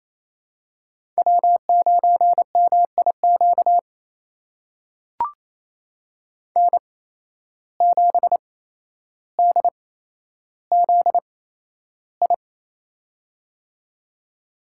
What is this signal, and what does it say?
Morse code, heard as a keyed tone, about 28 wpm, spaced as for 12 wpm, 700 Hz. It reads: W9MIQ N7DZI
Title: USA Call Signs - 5 Characters - 8x Character Spacing - 28wpm